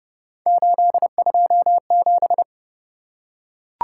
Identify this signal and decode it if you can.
Morse code, heard as a keyed tone, about 29 wpm, 700 Hz. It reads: 827